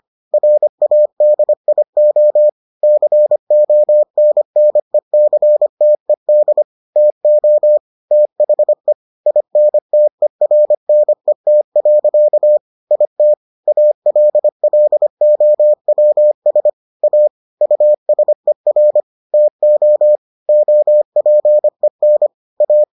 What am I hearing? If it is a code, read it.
Morse code, heard as a keyed tone, about 25 wpm, 600 Hz. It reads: RADIO CONNECTED TO THE INTERNET. IT ALLOWS A USER TO OPEN A